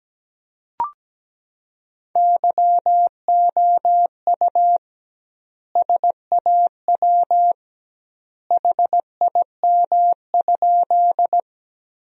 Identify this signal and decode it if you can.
Morse code, heard as a keyed tone, about 17 wpm, 700 Hz. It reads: YOU SAW HIM?
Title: 2x Word-Spacing - Sentences from Top 300 Words - 17wpm